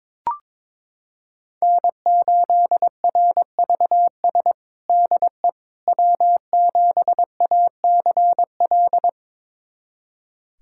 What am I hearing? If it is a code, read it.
Morse code, heard as a keyed tone, about 21 wpm, 700 Hz. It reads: N8RVS DE W7ACL
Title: USA Callsign DE Callsign - 5 Characters - 22wpm